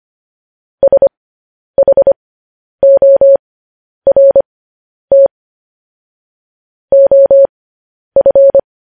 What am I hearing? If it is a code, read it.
Morse code, heard as a keyed tone, about 25 wpm, spaced as for 10 wpm, 550 Hz. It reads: SHORT OF
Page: news-headlines-in-morse-code-at-10-wpm